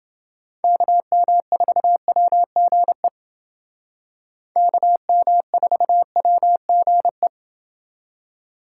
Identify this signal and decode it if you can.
Morse code, heard as a keyed tone, about 28 wpm, 700 Hz. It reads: KM4WGE KM4WGE